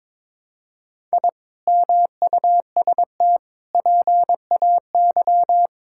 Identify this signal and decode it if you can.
Morse code, heard as a keyed tone, about 22 wpm, 700 Hz. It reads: I MUST PAY
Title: Sentences from Top 1000 Words - Review Part 6 - 22wpm